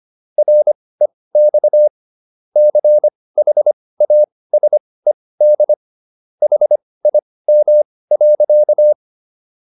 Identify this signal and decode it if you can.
Morse code, heard as a keyed tone, about 25 wpm, spaced as for 18 wpm, 600 Hz. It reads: REX CHASED HIM.